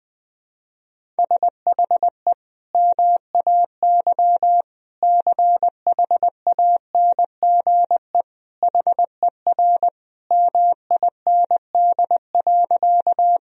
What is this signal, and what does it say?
Morse code, heard as a keyed tone, about 20 wpm, 700 Hz. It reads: SHE MAY CHANGE HER MIND.
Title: Sentences from Top 1000 Words - Review Part 1 - 20wpm